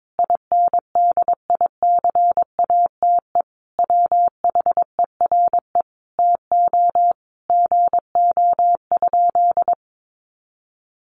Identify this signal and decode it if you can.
Morse code, heard as a keyed tone, about 22 wpm, 700 Hz. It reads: INDICATE WHERE TO GO?
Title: Sentences from Top 1300 Words 22wpm